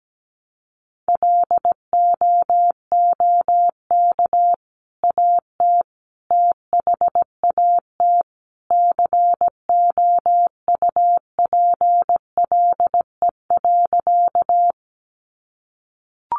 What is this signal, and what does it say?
Morse code, heard as a keyed tone, about 17 wpm, 700 Hz. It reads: LOOK AT THAT COUPLE.
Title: Sentences from Top 1100 Words 17wpm